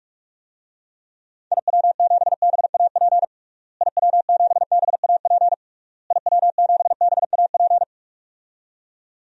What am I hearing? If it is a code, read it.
Morse code, heard as a keyed tone, about 46 wpm, 700 Hz. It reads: IW7BAP IW7BAP IW7BAP
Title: DXCC-100 Call Signs - 6 Characters - Repeated 3x - 45wpm